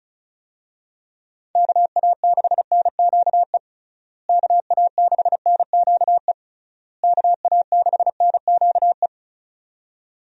Morse code, heard as a keyed tone, about 36 wpm, 700 Hz. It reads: KA6NQE KA6NQE KA6NQE
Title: USA Call Signs - 6 Characters - Repeated 3x - 35wpm